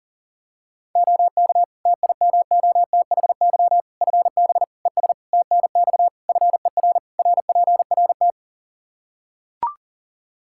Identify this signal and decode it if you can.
Morse code, heard as a keyed tone, about 40 wpm, 700 Hz. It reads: OK TIMOTHY FB ES TNX FER RPRT
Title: Practice QSO - Exchange 2 - Info - 40wpm